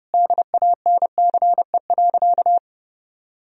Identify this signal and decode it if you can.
Morse code, heard as a keyed tone, about 32 wpm, 700 Hz. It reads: DANCE.
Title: Sentences from Top 1000 Words - Review Part 4 - 30wpm